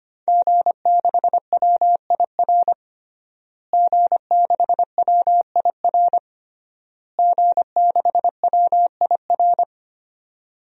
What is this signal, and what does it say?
Morse code, heard as a keyed tone, about 25 wpm, 700 Hz. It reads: G6WIR G6WIR G6WIR
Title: DXCC-100 Call Signs - 5 Characters - Repeated 3x - 25wpm